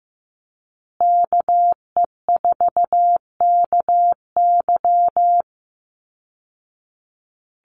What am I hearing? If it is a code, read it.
Morse code, heard as a keyed tone, about 15 wpm, 700 Hz. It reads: KE4KY
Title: Call Signs Repeated 2x - 15wpm